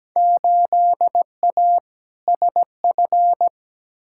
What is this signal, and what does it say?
Morse code, heard as a keyed tone, about 17 wpm, 700 Hz. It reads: 8A SF
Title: Field Day Exchanges Encore - 17wpm